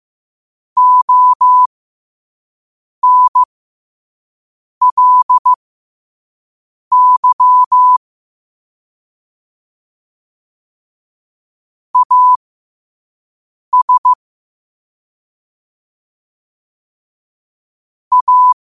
Morse code, heard as a keyed tone, about 15 wpm, spaced as for 5 wpm, 1000 Hz. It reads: ONLY AS A
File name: qotdfiles / 5wpm